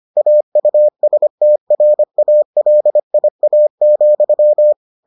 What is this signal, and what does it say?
Morse code, heard as a keyed tone, about 25 wpm, 600 Hz. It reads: AUSTRALIA,